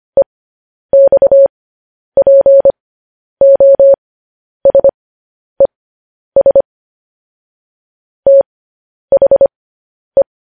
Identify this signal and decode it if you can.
Morse code, heard as a keyed tone, about 25 wpm, spaced as for 10 wpm, 550 Hz. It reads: EXPOSES THE
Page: news headlines in morse code at 10 wpm